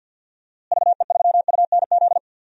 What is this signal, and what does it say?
Morse code, heard as a keyed tone, about 47 wpm, 700 Hz. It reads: VE3UNZ
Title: DXCC-100 Call Signs - 6 Characters - Repeated 2x - 50wpm